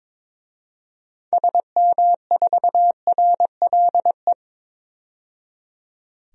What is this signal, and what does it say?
Morse code, heard as a keyed tone, about 22 wpm, 700 Hz. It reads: SM4RLE
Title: DXCC-100 Call Signs - 6 Characters - 22wpm